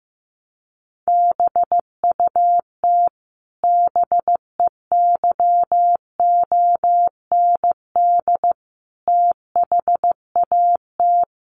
Morse code, heard as a keyed tone, about 15 wpm, 700 Hz. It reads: BUT BEYOND THAT
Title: Sentences from Top 1300 Words 15wpm